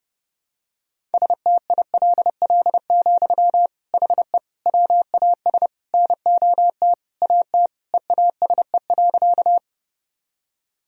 Morse code, heard as a keyed tone, about 30 wpm, 700 Hz. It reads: STILL, HE WAS NOT AT EASE.
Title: Sentences from Top 1000 Words - Review Part 1 - 30wpm